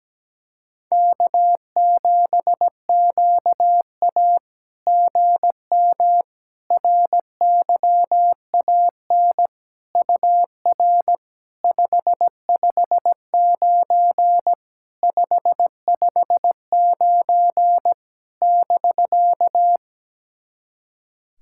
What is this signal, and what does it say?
Morse code, heard as a keyed tone, about 17 wpm, 700 Hz. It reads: K7QA GM RYAN UR 559 559 <BK>
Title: SOTA QSO Practice 17wpm